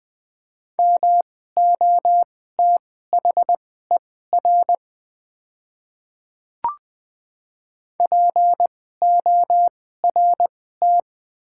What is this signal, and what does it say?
Morse code, heard as a keyed tone, about 20 wpm, spaced as for 14 wpm, 690 Hz. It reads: MOTHER PORT
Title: 2x Character Spacing - Top 500 Words - 20wpm